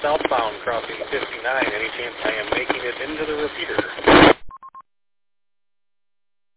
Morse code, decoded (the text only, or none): none